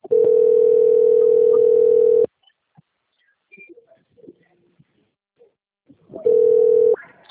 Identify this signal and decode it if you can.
Ringback tone (North America), 440 + 480 Hz.
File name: out- - - - - .94508